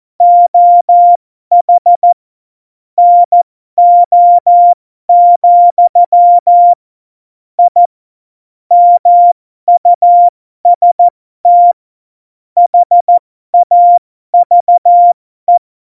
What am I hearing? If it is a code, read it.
Morse code, heard as a keyed tone, about 14 wpm, spaced as for 12 wpm, 700 Hz. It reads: OH NO, I MUST HAVE
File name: Morsekurs / Witze